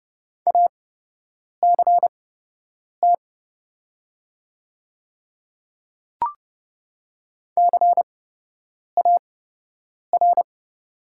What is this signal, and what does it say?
Morse code, heard as a keyed tone, about 29 wpm, spaced as for 8 wpm, 700 Hz. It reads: ACT CAR
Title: 8x Character Spacing - Top 300 Words - 30wpm